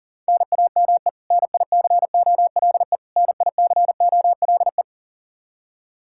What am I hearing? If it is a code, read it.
Morse code, heard as a keyed tone, about 37 wpm, 700 Hz. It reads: NAME NICOLE NICOLE
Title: Practice QSO - Exchange 1 - Beginning - 40wpm